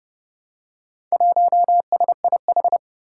Morse code, heard as a keyed tone, about 30 wpm, 700 Hz. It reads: 1SIH
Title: Four Challenging Letter-Number Combinations 30wpm